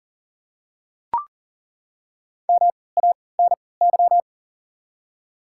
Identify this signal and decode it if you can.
Morse code, heard as a keyed tone, about 40 wpm, spaced as for 23 wpm, 700 Hz. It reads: MANY
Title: 3x Character Spacing - Top 100 Words - 40wpm